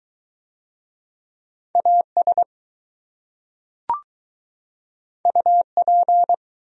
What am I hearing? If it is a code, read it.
Morse code, heard as a keyed tone, about 20 wpm, 700 Hz. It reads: AS UP